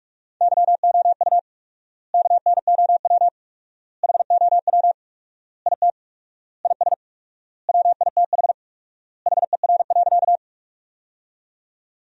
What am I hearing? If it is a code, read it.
Morse code, heard as a keyed tone, about 47 wpm, 700 Hz. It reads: YOU KNOW HOW IT IS WITH HER.
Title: 4x Word-Spacing - Sentences from Top 100 Words - 45wpm